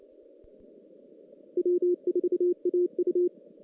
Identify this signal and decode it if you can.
Morse code, heard as a keyed tone, about 29 wpm, 360 Hz. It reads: W4AU
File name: 2016 JAN CW NAQP / 2016 Jan naqp k5ot - 0-5